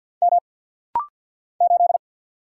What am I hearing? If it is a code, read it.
Morse code, heard as a keyed tone, about 49 wpm, 700 Hz. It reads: M 8
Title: Single Letter-Number - Rapid-Fire 50wpm